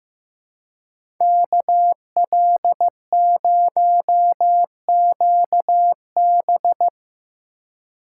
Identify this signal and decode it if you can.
Morse code, heard as a keyed tone, about 15 wpm, 700 Hz. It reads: KL0QB
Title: USA Call Signs - 5 Characters - 2x Character Spacing - 15wpm